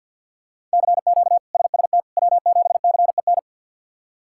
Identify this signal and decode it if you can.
Morse code, heard as a keyed tone, about 53 wpm, 700 Hz. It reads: CQ SST W7CEN
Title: K1USN SST QSO Practice - 50wpm